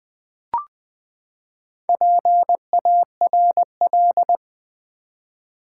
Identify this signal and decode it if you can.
Morse code, heard as a keyed tone, about 20 wpm, 700 Hz. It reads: PARL
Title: Build A Word - 4 Letters At A Time - 10-Letter Words - 20wpm